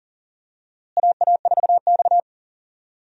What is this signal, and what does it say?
Morse code, heard as a keyed tone, about 37 wpm, 700 Hz. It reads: AA4X
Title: USA Call Signs - 4 Characters - Repeated 3x - 40wpm